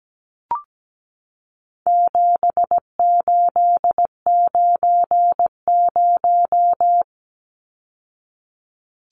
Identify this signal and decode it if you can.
Morse code, heard as a keyed tone, about 17 wpm, 700 Hz. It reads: 7890
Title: Numbers - 4-Digit 17wpm